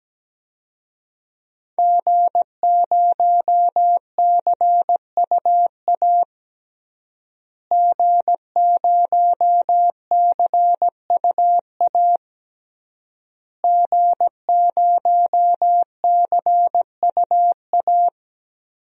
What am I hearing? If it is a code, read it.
Morse code, heard as a keyed tone, about 17 wpm, 700 Hz. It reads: G0CUA G0CUA G0CUA
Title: DXCC-100 Call Signs - 5 Characters - Repeated 3x - 17wpm